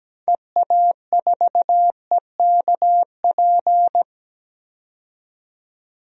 Morse code, heard as a keyed tone, about 17 wpm, 700 Hz. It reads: EA4EKP
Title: DXCC-100 Call Signs - 6 Characters - 17wpm